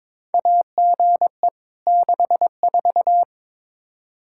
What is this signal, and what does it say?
Morse code, heard as a keyed tone, about 22 wpm, 700 Hz. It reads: AGE 64
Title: Practice QSO - Exchange 3 - More Info - 22wpm